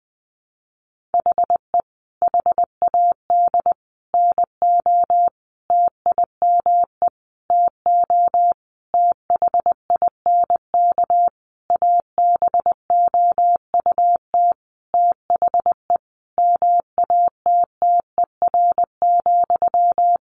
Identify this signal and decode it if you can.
Morse code, heard as a keyed tone, about 19 wpm, 700 Hz. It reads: HE HAD NO TIME TO THINK ABOUT THE MATTER,
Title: Sentences from Top 700 Words 20wpm